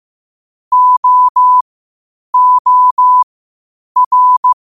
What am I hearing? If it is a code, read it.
Morse code, heard as a keyed tone, about 15 wpm, spaced as for 8 wpm, 1000 Hz. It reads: OOR
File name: qotdfiles / 8wpm